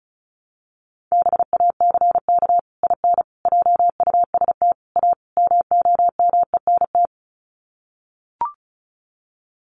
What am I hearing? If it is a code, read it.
Morse code, heard as a keyed tone, about 34 wpm, 700 Hz. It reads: BACK IN JUST A MOMENT
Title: Sets of 5 Words Encore 35wpm